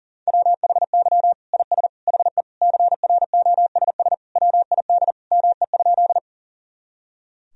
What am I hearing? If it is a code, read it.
Morse code, heard as a keyed tone, about 40 wpm, 700 Hz. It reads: WHY IS HE CROSS WID ME?
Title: Sentences from Top 300 Words and Abbreviations 40wpm